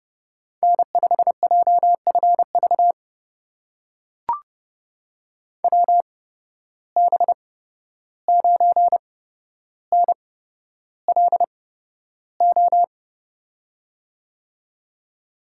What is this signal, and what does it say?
Morse code, heard as a keyed tone, about 32 wpm, spaced as for 12 wpm, 700 Hz. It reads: N5JFV WB9NLO
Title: USA Call Signs - 8x Character Spacing - 30wpm